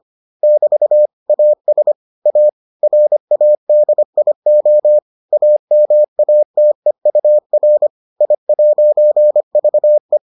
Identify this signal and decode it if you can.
Morse code, heard as a keyed tone, about 25 wpm, 600 Hz. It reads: <BT> AS A RADIO AMATEUR I'VE